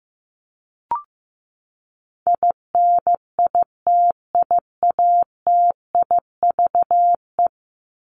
Morse code, heard as a keyed tone, about 15 wpm, 700 Hz. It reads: INITIATIVE
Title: Top 1751-2000 Words 15wpm